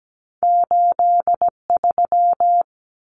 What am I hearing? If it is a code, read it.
Morse code, heard as a keyed tone, about 17 wpm, 700 Hz. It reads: 83
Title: Numbers - 2-Digit 17wpm